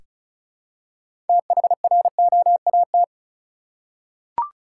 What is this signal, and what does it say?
Morse code, heard as a keyed tone, about 36 wpm, 700 Hz. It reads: THROAT
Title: Six Letter Words - 35wpm